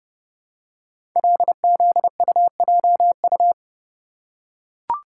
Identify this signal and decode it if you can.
Morse code, heard as a keyed tone, about 30 wpm, 700 Hz. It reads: LZUJU
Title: Code Groups - 5 Letters - 30wpm